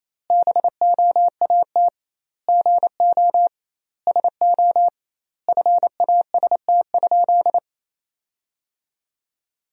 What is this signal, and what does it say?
Morse code, heard as a keyed tone, about 28 wpm, 700 Hz. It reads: BOAT GO SO FAST?
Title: 2x Word-Spacing - Sentences from Top 500 Words - 28wpm